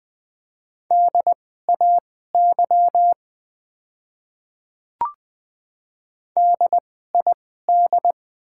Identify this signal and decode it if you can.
Morse code, heard as a keyed tone, about 20 wpm, spaced as for 14 wpm, 700 Hz. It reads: DAY DID
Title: 2x Character Spacing - Top 100 Words - 20wpm